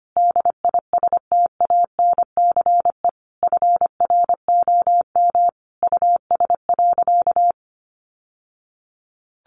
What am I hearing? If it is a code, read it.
Morse code, heard as a keyed tone, about 25 wpm, 700 Hz. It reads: DISTANCE FROM US.
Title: Sentences from Top 1200 Words 25wpm